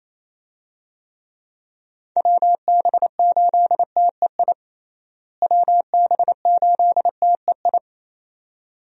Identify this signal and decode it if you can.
Morse code, heard as a keyed tone, about 28 wpm, 700 Hz. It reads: WB8TEI WB8TEI
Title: USA Call Signs - 6 Characters - Repeated 2x - 28wpm